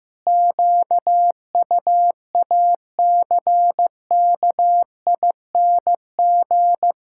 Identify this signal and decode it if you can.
Morse code, heard as a keyed tone, about 16 wpm, 700 Hz. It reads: QUACKING
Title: Pangram Challenge v2 - 15wpm